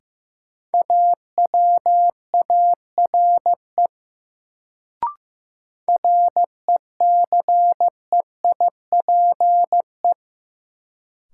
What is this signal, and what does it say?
Morse code, heard as a keyed tone, about 15 wpm, 700 Hz. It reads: AWARE RECEIPE
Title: Sentences from Top 1200 Words - New Word Drill - 15wpm